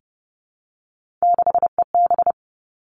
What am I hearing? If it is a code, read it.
Morse code, heard as a keyed tone, about 30 wpm, 700 Hz. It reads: 6EB
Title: Three Challenging Letter-Number Combinations 30wpm